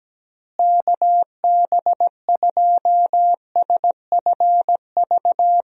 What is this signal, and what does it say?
Morse code, heard as a keyed tone, about 17 wpm, 700 Hz. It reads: KB2SFV
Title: USA Call Signs - 6 Characters - Repeated 2x - 17wpm